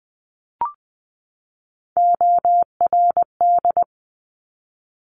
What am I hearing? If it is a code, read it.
Morse code, heard as a keyed tone, about 20 wpm, 700 Hz. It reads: ORD
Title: Top Three Letter Combinations in Words 20wpm